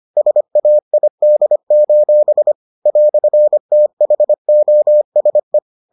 Morse code, heard as a keyed tone, about 25 wpm, 600 Hz. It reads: SAID: "THOSE